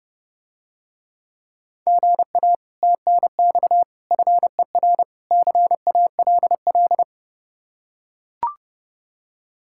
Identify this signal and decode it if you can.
Morse code, heard as a keyed tone, about 30 wpm, 700 Hz. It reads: GA TNX FER CALL